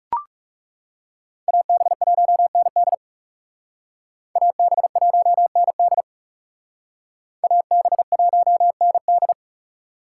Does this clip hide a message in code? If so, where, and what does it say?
Yes: Morse code from 1.5 to 9.3 s, heard as a keyed tone, about 38 wpm, 700 Hz. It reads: AB1ND AB1ND AB1ND